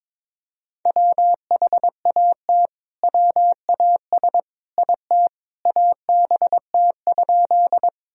Morse code, heard as a keyed tone, about 22 wpm, 700 Hz. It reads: WHAT WAS IT ABT?
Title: Sentences from Top 100 Words and Abbreviations 22wpm